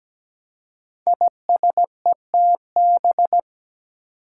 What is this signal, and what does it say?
Morse code, heard as a keyed tone, about 17 wpm, 700 Hz. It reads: ISETB
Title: Five Challenging Letter-Number Combinations 17wpm